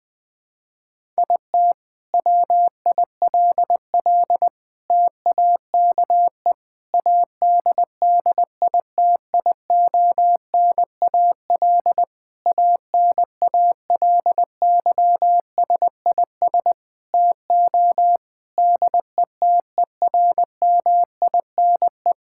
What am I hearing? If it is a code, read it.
Morse code, heard as a keyed tone, about 20 wpm, 700 Hz. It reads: IT WILL TAKE ADDITIONAL ANALYSIS TO DETERMINE
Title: Sentences from Top 1000 Words - Review Part 5 - 20wpm